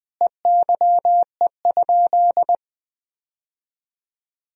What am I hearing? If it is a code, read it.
Morse code, heard as a keyed tone, about 20 wpm, 700 Hz. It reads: EYE?
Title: Sentences from Top 1000 Words - Review Part 1 - 20wpm